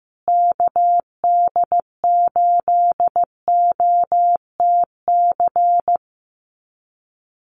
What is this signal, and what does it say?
Morse code, heard as a keyed tone, about 14 wpm, 700 Hz. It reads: KD8OTC
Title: USA Call Signs Repeated 3x 15wpm